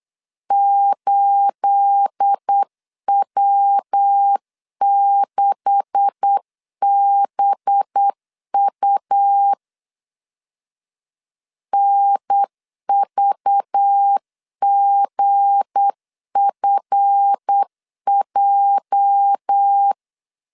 Morse code, heard as a keyed tone, about 8 wpm, 800 Hz. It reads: 8W6BU NVGFJ
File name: LEZIONI / 9 286 8wpm 3425